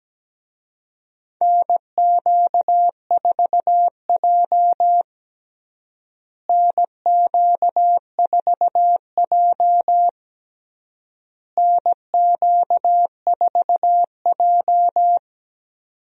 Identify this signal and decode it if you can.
Morse code, heard as a keyed tone, about 17 wpm, 700 Hz. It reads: NQ4J NQ4J NQ4J
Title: USA Call Signs - 4 Characters - Repeated 3x - 17wpm